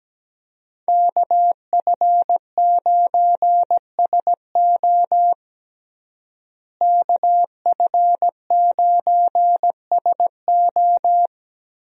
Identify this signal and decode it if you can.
Morse code, heard as a keyed tone, about 17 wpm, 700 Hz. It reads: KF9SO KF9SO
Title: USA Call Signs - 5 Characters - Repeated 3x - 17wpm